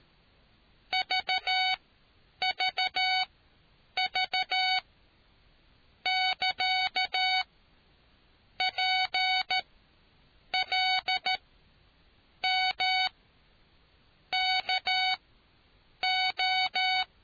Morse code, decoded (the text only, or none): VVV <KA> PLMKO